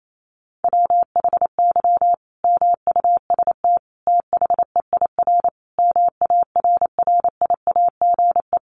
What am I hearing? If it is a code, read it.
Morse code, heard as a keyed tone, about 28 wpm, 700 Hz. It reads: WHY MUST THEIR MARRIAGE
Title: Sentences from Top 1000 Words 28wpm